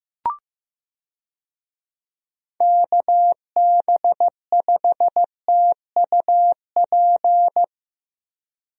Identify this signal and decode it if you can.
Morse code, heard as a keyed tone, about 15 wpm, 700 Hz. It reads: KB5TUP
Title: USA Call Signs - 6 Characters - Repeated 2x - 15wpm